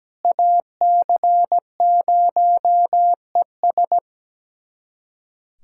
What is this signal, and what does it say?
Morse code, heard as a keyed tone, about 17 wpm, 700 Hz. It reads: AC0ES
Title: USA Callsign DE Callsign 17wpm